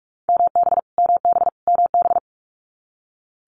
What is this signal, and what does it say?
Morse code, heard as a keyed tone, about 46 wpm, 700 Hz. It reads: MB MB MB
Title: Multi-letter Phonemes Repeated 3x 45wpm